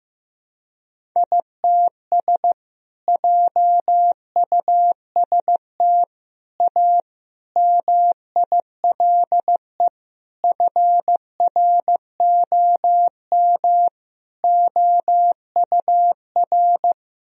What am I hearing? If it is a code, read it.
Morse code, heard as a keyed tone, about 16 wpm, 700 Hz. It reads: ITS JUST A MILE FROM OUR
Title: Sentences from Top 1000 Words - Review Part 1 - 15wpm